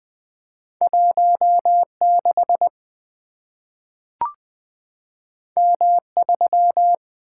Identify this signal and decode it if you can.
Morse code, heard as a keyed tone, about 20 wpm, 700 Hz. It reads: 16 M3
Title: Two Letter-Number Combinations - 20wpm